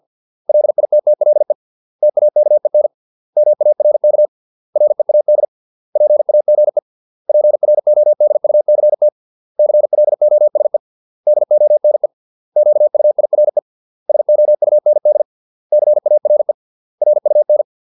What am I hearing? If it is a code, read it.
Morse code, heard as a keyed tone, about 50 wpm, 600 Hz. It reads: LITTLE TAKEN MARK READ PAGE PRODUCT CLOSE DONE QUIRE SOUND CARE RUN